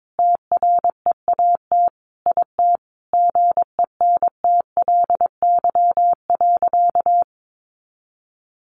Morse code, heard as a keyed tone, about 22 wpm, 700 Hz. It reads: TREAT IT GENTLY.
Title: Sentences from Top 1500 Words 22wpm